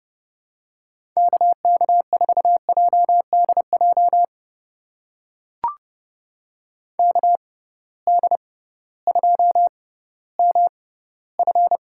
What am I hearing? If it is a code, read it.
Morse code, heard as a keyed tone, about 32 wpm, spaced as for 17 wpm, 700 Hz. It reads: KK4JDJ KD2MF
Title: USA Call Signs - 6 Characters - 6x Character Spacing - 30wpm